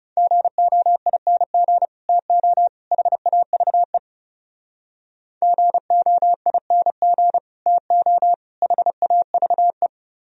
Morse code, heard as a keyed tone, about 32 wpm, 700 Hz. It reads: GOING TO HAVE GOING TO HAVE
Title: Speed-Racing - Sets of 3 Words 30wpm